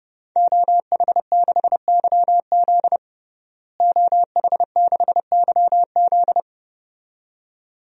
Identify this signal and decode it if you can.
Morse code, heard as a keyed tone, about 32 wpm, 700 Hz. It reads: OH6YZ OH6YZ
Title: DXCC-100 Call Signs - 5 Characters - Repeated 2x - 30wpm